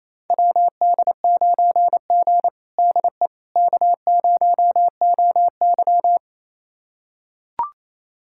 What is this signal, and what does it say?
Morse code, heard as a keyed tone, about 28 wpm, 700 Hz. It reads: WD9G DE K0OY
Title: USA Callsign DE Callsign - 4 Characters - 28wpm